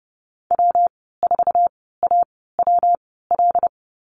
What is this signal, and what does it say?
Morse code, heard as a keyed tone, about 30 wpm, spaced as for 17 wpm, 700 Hz. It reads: W4AWL
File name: Lesson 19g - Letter w - Call Signs 30wpm